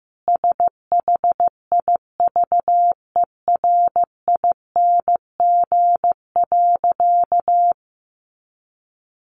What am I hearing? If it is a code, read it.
Morse code, heard as a keyed tone, about 15 wpm, 700 Hz. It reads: SHIVERING.
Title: A Sound Of Thunder 15wpm